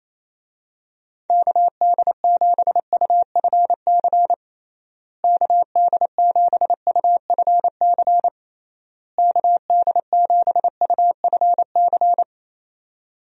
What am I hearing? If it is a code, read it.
Morse code, heard as a keyed tone, about 28 wpm, 700 Hz. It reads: KD7UFC KD7UFC KD7UFC
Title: USA Call Signs - 6 Characters - Repeated 3x - 28wpm